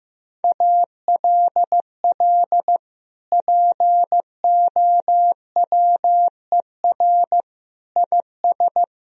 Morse code, heard as a keyed tone, about 15 wpm, 700 Hz. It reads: ALL POWER IS
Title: Sentences from Top 1000 Words - Review Part 4 - 15wpm